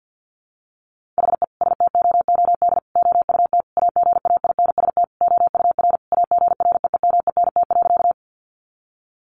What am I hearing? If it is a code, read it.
Morse code, heard as a keyed tone, about 48 wpm, 700 Hz. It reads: HE STOOD OUT AGAINST OUR AGREEMENT.
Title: Sentences from Top 1500 Words 50wpm